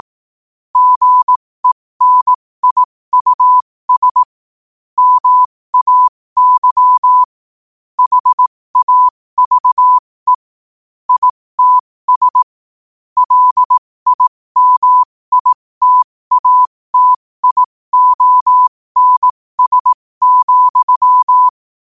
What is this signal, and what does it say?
Morse code, heard as a keyed tone, about 18 wpm, spaced as for 15 wpm, 1000 Hz. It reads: GENIUS MAY HAVE ITS LIMITATIONS,
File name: qotdfiles / 15wpm